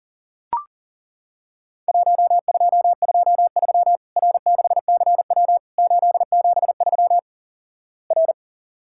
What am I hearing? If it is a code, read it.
Morse code, heard as a keyed tone, about 37 wpm, 700 Hz. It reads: 1223 R6CW 873 R
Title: Worked All Europe DX Contest - QTC Exchange - 40wpm